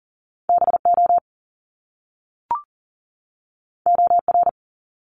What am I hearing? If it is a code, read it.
Morse code, heard as a keyed tone, about 42 wpm, 700 Hz. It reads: BO OR
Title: Top Two Letter Combinations in Words 40wpm